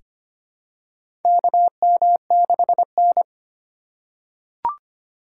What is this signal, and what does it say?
Morse code, heard as a keyed tone, about 25 wpm, 700 Hz. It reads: KM6N
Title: USA Call Signs - 4 Characters - 3x Character Spacing - 25wpm